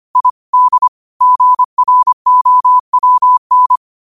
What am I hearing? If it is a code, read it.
Morse code, heard as a keyed tone, about 25 wpm, 1000 Hz. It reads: I D GROWN
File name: 25wpm